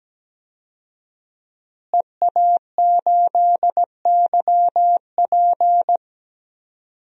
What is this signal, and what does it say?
Morse code, heard as a keyed tone, about 17 wpm, 700 Hz. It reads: EA8YP
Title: DXCC-100 Call Signs - 5 Characters - Repeated 3x - 17wpm